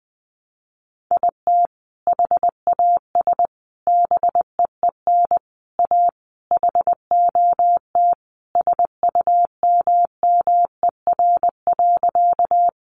Morse code, heard as a keyed tone, about 20 wpm, 700 Hz. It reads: IT HAS BEEN A HOT SUMMER.
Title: Sentences from Top 1100 Words 20wpm